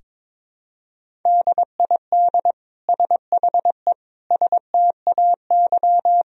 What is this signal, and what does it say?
Morse code, heard as a keyed tone, about 22 wpm, 700 Hz. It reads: DID SHE STAY
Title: Sentences from Top 500 Words and Abbreviations 22wpm